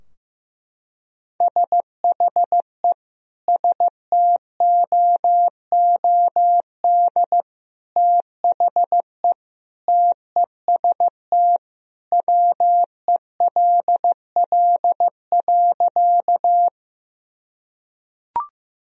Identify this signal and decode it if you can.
Morse code, heard as a keyed tone, about 15 wpm, 700 Hz. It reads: SHE STOOD THE TEST WELL.
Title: Sentences from Top 1000 Words - Review Part 3 - 15wpm